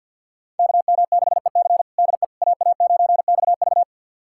Morse code, heard as a keyed tone, about 51 wpm, 700 Hz. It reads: KM6EC DE AA9XV
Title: USA Callsign DE Callsign - 5 Characters - 50wpm